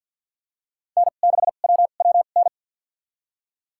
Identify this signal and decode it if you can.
Morse code, heard as a keyed tone, about 50 wpm, spaced as for 36 wpm, 700 Hz. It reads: N6WWN
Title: USA Call Signs - 5 Characters - 2x Character Spacing - 50wpm